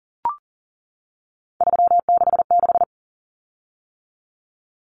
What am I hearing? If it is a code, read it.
Morse code, heard as a keyed tone, about 37 wpm, 700 Hz. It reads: 366
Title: Common Cut Numbers - 3-Digit 40wpm